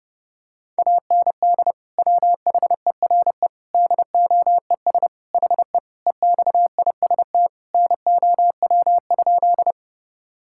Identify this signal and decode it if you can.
Morse code, heard as a keyed tone, about 32 wpm, 700 Hz. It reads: AND WHERE DOES HE EXIST NOW?
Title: Sentences from Top 1000 Words - Review Part 5 - 30wpm